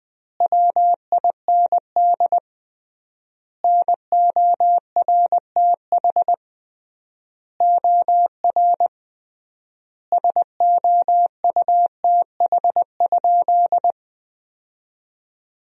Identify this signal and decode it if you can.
Morse code, heard as a keyed tone, about 20 wpm, 700 Hz. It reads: WIND NORTH OR SOUTH?
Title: 3x Word-Spacing - Sentences from Top 400 Words - 20wpm